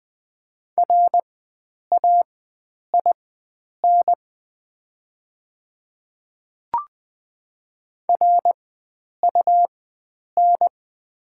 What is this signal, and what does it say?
Morse code, heard as a keyed tone, about 20 wpm, spaced as for 9 wpm, 700 Hz. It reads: RAIN RUN
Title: 4x Character Spacing - Top 300 Words - 20wpm